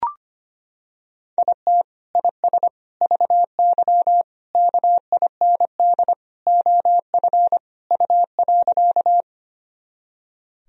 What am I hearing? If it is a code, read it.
Morse code, heard as a keyed tone, about 25 wpm, 700 Hz. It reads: IT IS VY KIND OF U.
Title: Sentences from Top 200 Words and Abbreviations 25wpm